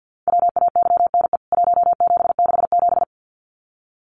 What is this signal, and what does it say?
Morse code, heard as a keyed tone, about 50 wpm, 700 Hz. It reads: WAYNE 1767
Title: CWT Exchange 50wpm